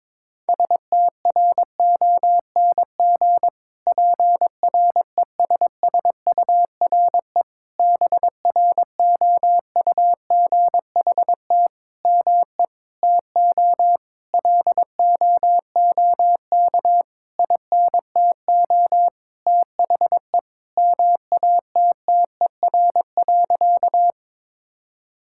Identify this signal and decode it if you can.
Morse code, heard as a keyed tone, about 22 wpm, 700 Hz. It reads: STRONG PRESSURE BROUGHT ME TO LOOK INTO THE MATTER.
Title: Sentences from Top 1000 Words - Review Part 6 - 22wpm